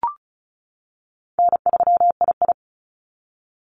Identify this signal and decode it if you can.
Morse code, heard as a keyed tone, about 34 wpm, 700 Hz. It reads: N3II
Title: NA SOTA Activator Callsigns 35wpm